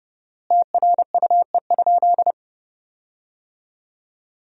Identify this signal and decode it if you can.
Morse code, heard as a keyed tone, about 30 wpm, 700 Hz. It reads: TRUE?
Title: 4x Word-Spacing - Sentences from Top 400 Words - 30wpm